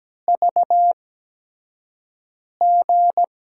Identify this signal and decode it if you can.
Morse code, heard as a keyed tone, about 17 wpm, spaced as for 5 wpm, 700 Hz. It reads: VG
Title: USA Call Signs - 6 Characters - 8x Character Spacing - 17wpm